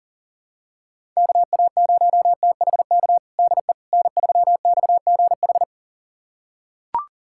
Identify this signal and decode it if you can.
Morse code, heard as a keyed tone, about 40 wpm, 700 Hz. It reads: KA0THK DE N3XGH
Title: USA Callsign DE Callsign 40wpm